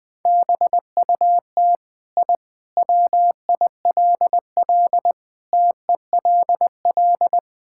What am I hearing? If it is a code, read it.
Morse code, heard as a keyed tone, about 19 wpm, 700 Hz. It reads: BUT I WILL TELL
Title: Sentences from Top 1000 Words - Review Part 5 - 20wpm